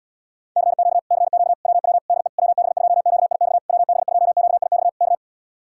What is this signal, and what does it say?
Morse code, heard as a keyed tone, about 77 wpm, 700 Hz. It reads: CQ CQ WW DE WB2QIX WB2QIX K